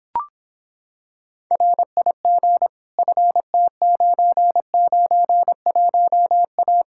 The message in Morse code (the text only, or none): RIG FT991A